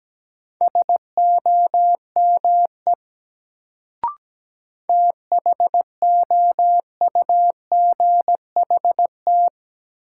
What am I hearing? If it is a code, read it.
Morse code, heard as a keyed tone, about 17 wpm, 700 Hz. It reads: SOME THOUGHT
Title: Sets of 2 Words - New Word Drill - 17wpm